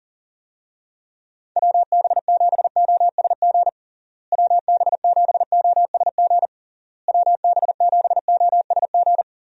Morse code, heard as a keyed tone, about 37 wpm, 700 Hz. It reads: WB7OSG WB7OSG WB7OSG
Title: USA Call Signs - 6 Characters - Repeated 3x - 40wpm